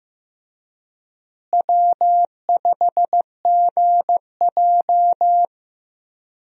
Morse code, heard as a keyed tone, about 15 wpm, 700 Hz. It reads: W5GJ
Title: USA Call Signs - 4 Characters - Repeated 3x - 15wpm